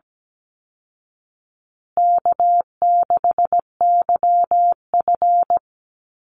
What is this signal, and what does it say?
Morse code, heard as a keyed tone, about 17 wpm, 700 Hz. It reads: K6YF
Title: Easy Call Signs Repeated 2x - 17wpm